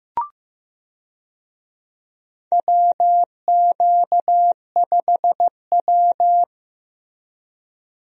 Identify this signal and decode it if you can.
Morse code, heard as a keyed tone, about 15 wpm, 700 Hz. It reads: WQ5W
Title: USA Call Signs - 4 Characters - Repeated 2x - 15wpm